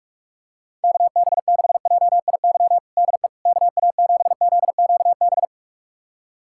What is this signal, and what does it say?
Morse code, heard as a keyed tone, about 46 wpm, 700 Hz. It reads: KB6JIY DE KA7ZQB
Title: USA Callsign DE Callsign - 6 Characters - 45wpm